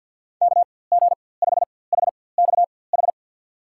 Morse code, heard as a keyed tone, about 48 wpm, spaced as for 23 wpm, 700 Hz. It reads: KG5HXH
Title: USA Call Signs - 6 Characters - 4x Character Spacing - 50wpm